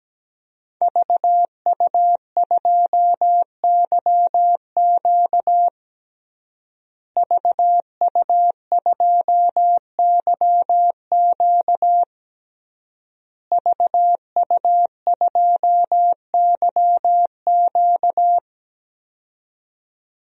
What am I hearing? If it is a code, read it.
Morse code, heard as a keyed tone, about 17 wpm, 700 Hz. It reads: VU2YQ VU2YQ VU2YQ
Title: DXCC-100 Call Signs - 5 Characters - Repeated 3x - 17wpm